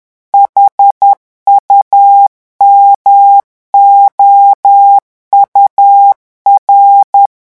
Morse code, heard as a keyed tone, about 10 wpm, 800 Hz. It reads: HUMOUR
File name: morse mp3 / revision francais 10 mots minute